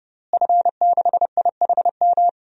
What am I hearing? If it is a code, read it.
Morse code, heard as a keyed tone, about 30 wpm, 700 Hz. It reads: F6IHM
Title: DXCC-100 Call Signs - 5 Characters - Repeated 3x - 30wpm